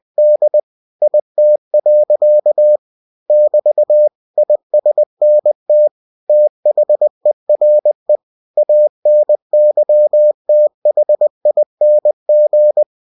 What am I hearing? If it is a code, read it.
Morse code, heard as a keyed tone, about 20 wpm, 600 Hz. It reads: D IT. <BT> ISNT THERE ANYTHING
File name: cw-books / winnie-the-pooh / winnie-20wpm-page-0098